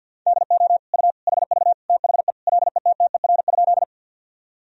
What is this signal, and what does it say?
Morse code, heard as a keyed tone, about 52 wpm, 700 Hz. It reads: DO U HV THE LETTER?
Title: Sentences from Top 300 Words and Abbreviations 50wpm